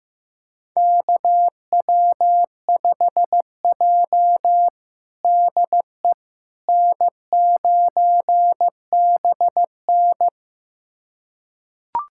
Morse code, heard as a keyed tone, about 15 wpm, 700 Hz. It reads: KW5J DE N9BN
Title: USA Callsign DE Callsign - 4 Characters - 15wpm